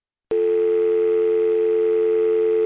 Dial tone (North America), 350 + 440 Hz.